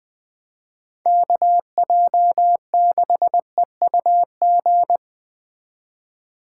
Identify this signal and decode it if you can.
Morse code, heard as a keyed tone, about 20 wpm, 700 Hz. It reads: KJ6EUG
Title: Speed Racing - USA 2x3 Call Signs 17wpm